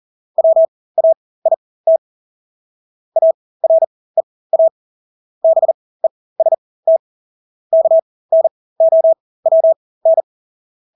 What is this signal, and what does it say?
Morse code, heard as a keyed tone, about 40 wpm, spaced as for 20 wpm, 650 Hz. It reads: WAIT AREA BEST KNOWN
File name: Engelske ord 40-20 wpm